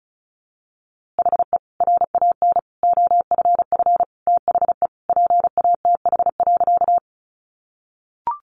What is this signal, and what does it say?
Morse code, heard as a keyed tone, about 34 wpm, 700 Hz. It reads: HE RAN OFF THE PATH.
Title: A Sound Of Thunder 35wpm